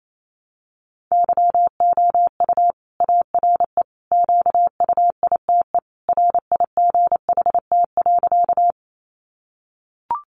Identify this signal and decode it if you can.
Morse code, heard as a keyed tone, about 28 wpm, 700 Hz. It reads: YOU ARE QUITE RIGHT.
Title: Sentences from Top 700 Words 28wpm